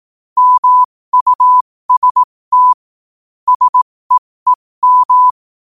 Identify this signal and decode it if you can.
Morse code, heard as a keyed tone, about 18 wpm, spaced as for 15 wpm, 1000 Hz. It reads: MUST SEEM
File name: qotdfiles / 15wpm